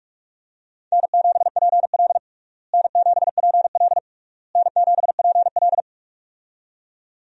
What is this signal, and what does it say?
Morse code, heard as a keyed tone, about 46 wpm, 700 Hz. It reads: N7PL N7PL N7PL
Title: USA Call Signs - 4 Characters - Repeated 3x - 45wpm